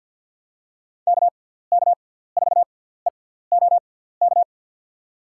Morse code, heard as a keyed tone, about 52 wpm, spaced as for 17 wpm, 700 Hz. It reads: KK4EOK
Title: USA Call Signs - 6x Character Spacing - 50wpm